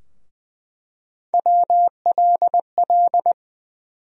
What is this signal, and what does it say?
Morse code, heard as a keyed tone, about 20 wpm, 700 Hz. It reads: WLL
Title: USA Call Sign Postfixes - 3 Letters 20wpm